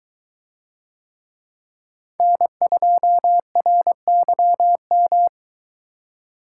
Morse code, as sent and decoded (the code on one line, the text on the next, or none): -. ..--- .-. -.-- --
N2RYM